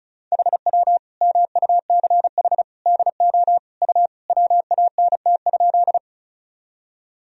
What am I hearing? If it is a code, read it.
Morse code, heard as a keyed tone, about 36 wpm, 700 Hz. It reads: HW MUCH DO U WANT?
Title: Sentences from Top 200 Words and Abbreviations 35wpm